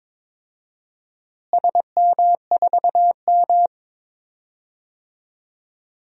Morse code, heard as a keyed tone, about 22 wpm, 700 Hz. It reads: SM4M
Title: DXCC-100 Call Signs - 4 Characters - Repeated 2x - 22wpm